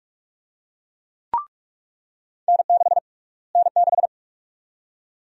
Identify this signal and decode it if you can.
Morse code, heard as a keyed tone, about 42 wpm, 700 Hz. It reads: N6 N6
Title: USA Call Sign Prefixes - Repeated 2x - 45wpm